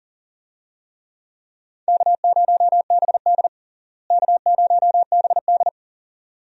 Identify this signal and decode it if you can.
Morse code, heard as a keyed tone, about 40 wpm, 700 Hz. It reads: K0BD K0BD
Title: USA Call Signs - 4 Characters - Repeated 2x - 40wpm